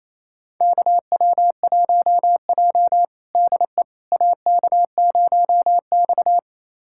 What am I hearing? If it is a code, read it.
Morse code, heard as a keyed tone, about 28 wpm, 700 Hz. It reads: KW1J DE AK0X
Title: USA Callsign DE Callsign - 4 Characters - 28wpm